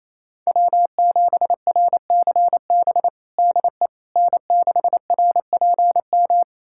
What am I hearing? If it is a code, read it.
Morse code, heard as a keyed tone, about 28 wpm, 700 Hz. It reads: W7RCB DE N6RPM
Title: USA Callsign DE Callsign - 5 Characters - 28wpm